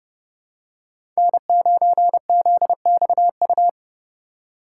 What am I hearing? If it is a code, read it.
Morse code, heard as a keyed tone, about 30 wpm, 700 Hz. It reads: N9ZXU
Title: USA Call Signs - 3x Character Spacing - 30wpm